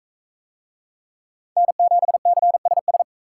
Morse code, heard as a keyed tone, about 44 wpm, 700 Hz. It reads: N7CSS